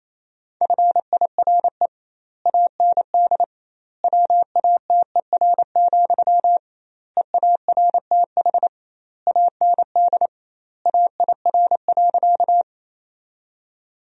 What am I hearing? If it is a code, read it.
Morse code, heard as a keyed tone, about 27 wpm, 700 Hz. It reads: FIRE AND WATER, EARTH AND AIR.
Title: 2x Word-Spacing - Sentences from Top 400 Words - 28wpm